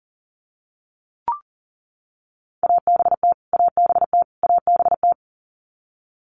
Morse code, heard as a keyed tone, about 40 wpm, 700 Hz. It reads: ABT ABT ABT
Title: Abbreviations Repeated 3x - 40wpm